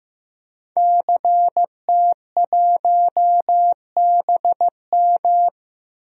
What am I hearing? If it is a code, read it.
Morse code, heard as a keyed tone, about 15 wpm, 700 Hz. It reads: CT1BM
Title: DXCC-100 Call Signs - 5 Characters - Repeated 2x - 15wpm